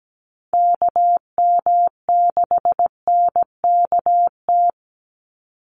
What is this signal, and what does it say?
Morse code, heard as a keyed tone, about 17 wpm, 700 Hz. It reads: KM6NKT
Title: USA Call Signs 17wpm